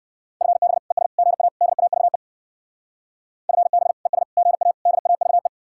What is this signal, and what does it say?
Morse code, heard as a keyed tone, about 64 wpm, 700 Hz. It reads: FB ES GA DAVE FB ES GA DAVE